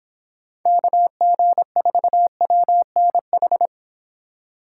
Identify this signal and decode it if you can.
Morse code, heard as a keyed tone, about 26 wpm, 700 Hz. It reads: KG4WNH